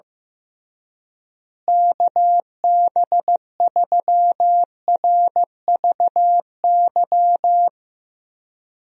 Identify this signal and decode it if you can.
Morse code, heard as a keyed tone, about 15 wpm, 700 Hz. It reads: KB3RVY
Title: USA Call Signs - 6 Characters - Repeated 3x - 15wpm